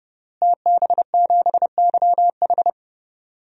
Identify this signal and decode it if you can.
Morse code, heard as a keyed tone, about 29 wpm, 700 Hz. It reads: TB7YH
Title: DXCC-100 Call Signs - 5 Characters - Repeated 3x - 30wpm